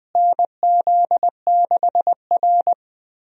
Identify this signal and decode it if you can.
Morse code, heard as a keyed tone, about 20 wpm, 700 Hz. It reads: NZ6R
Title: USA Call Signs - 4 Characters - Repeated 2x - 20wpm